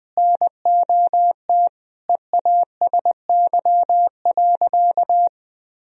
Morse code, heard as a keyed tone, about 20 wpm, 700 Hz. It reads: NOT EASY.
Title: Sentences from Top 1000 Words - Review Part 2 - 20wpm